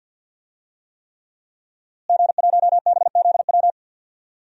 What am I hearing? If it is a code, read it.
Morse code, heard as a keyed tone, about 50 wpm, 700 Hz. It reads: G1BZW
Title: DXCC-100 Call Signs - 5 Characters - Repeated 3x - 50wpm